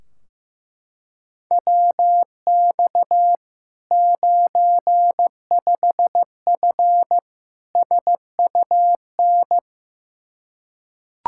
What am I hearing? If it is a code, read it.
Morse code, heard as a keyed tone, about 16 wpm, 700 Hz. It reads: WX 95F SUN
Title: Practice QSO - Exchange 2 - Info - 15wpm